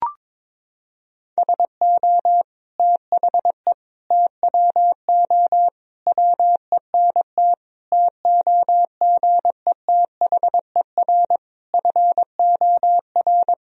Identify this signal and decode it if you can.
Morse code, heard as a keyed tone, about 21 wpm, 700 Hz. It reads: SO THE TWO WENT TOGETHER FOR
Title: Sentences from Top 1000 Words - Review Part 1 - 22wpm